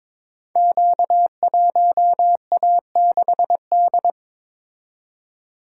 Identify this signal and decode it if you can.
Morse code, heard as a keyed tone, about 22 wpm, 700 Hz. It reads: Q1A6D
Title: Code Groups - 5 Letter:Numbers - 22wpm